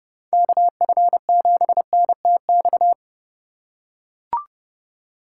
Morse code, heard as a keyed tone, about 29 wpm, 700 Hz. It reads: KF7NTX
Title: USA Call Signs - 3x Character Spacing - 30wpm